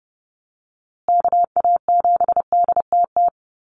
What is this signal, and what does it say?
Morse code, heard as a keyed tone, about 30 wpm, 700 Hz. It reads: KA7DTT
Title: USA Call Signs Repeated 2x 30wpm